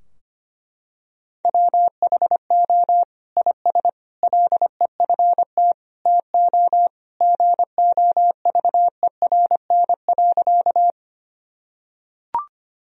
Morse code, heard as a keyed tone, about 25 wpm, 700 Hz. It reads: WHO IS LEFT TO GOVERN.
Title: Sentences from Top 1000 Words - Review Part 2 - 25wpm